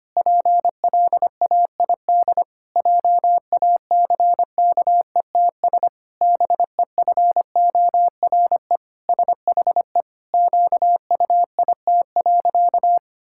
Morse code, heard as a keyed tone, about 25 wpm, 700 Hz. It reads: PLAID JACKETS BEFORE SHE QUIT.
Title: Pangram Challenge v2 - 25wpm